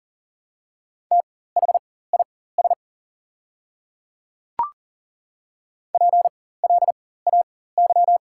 Morse code, heard as a keyed tone, about 40 wpm, spaced as for 19 wpm, 700 Hz. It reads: THIS PLAY
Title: 4x Character Spacing - Top 500 Words - 40wpm